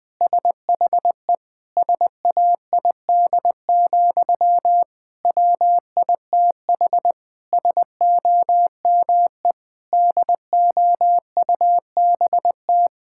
Morse code, heard as a keyed tone, about 20 wpm, 700 Hz. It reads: SHE SAID, WITH SOME DOUBT
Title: Sentences from Top 1600 Words 20wpm